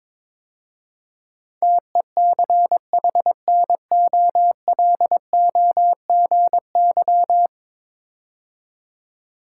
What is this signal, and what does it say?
Morse code, heard as a keyed tone, about 22 wpm, 700 Hz. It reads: TECHNOLOGY
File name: practice / 22wpm / 10 Letter Words - First 3 Letters Given - 22wpm